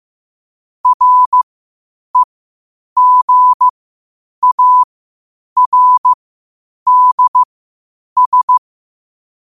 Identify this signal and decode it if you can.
Morse code, heard as a keyed tone, about 15 wpm, spaced as for 8 wpm, 1000 Hz. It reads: REGARDS
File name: qotdfiles / 8wpm